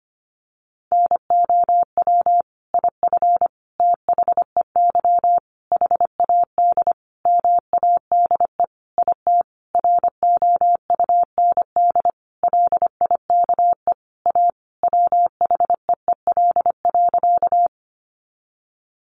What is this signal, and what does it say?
Morse code, heard as a keyed tone, about 25 wpm, 700 Hz. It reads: NOW IF THEY HAD MADE IT ROUND LIKE A WHEEL.
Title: Sentences from Top 500 Words 25wpm